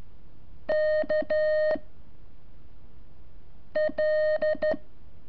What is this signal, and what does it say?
Morse code, heard as a keyed tone, about 13 wpm, 620 Hz. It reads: K L
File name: curso / ejercicio 25